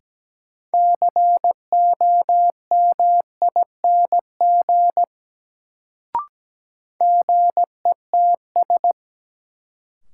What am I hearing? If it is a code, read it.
Morse code, heard as a keyed tone, about 17 wpm, 700 Hz. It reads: COMING GETS
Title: Sentences from Top 1300 Words - New Word Drill - 17wpm